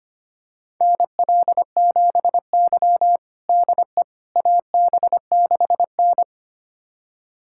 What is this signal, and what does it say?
Morse code, heard as a keyed tone, about 25 wpm, 700 Hz. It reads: NL7Y DE AB6N
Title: USA Callsign DE Callsign - 4 Characters - 25wpm